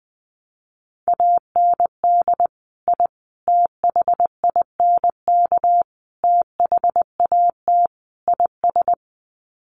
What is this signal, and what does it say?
Morse code, heard as a keyed tone, about 20 wpm, 700 Hz. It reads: AND I THINK THAT IS
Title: Sets of 5 Words Encore2 20wpm